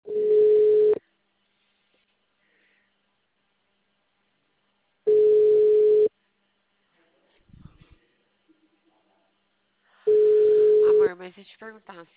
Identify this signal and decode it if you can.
Ringback tone (Europe, 425 Hz), with speech.
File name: out- -2007- - - .13513